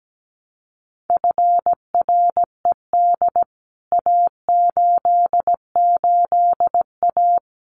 Morse code, heard as a keyed tone, about 17 wpm, 700 Hz. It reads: FRED A88A
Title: CWT Exchange 17wpm